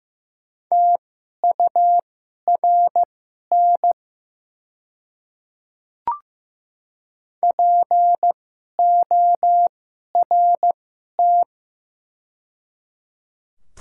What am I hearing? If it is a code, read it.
Morse code, heard as a keyed tone, about 15 wpm, spaced as for 11 wpm, 700 Hz. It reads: TURN PORT
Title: 2x Character Spacing - Top 200 Words - 15wpm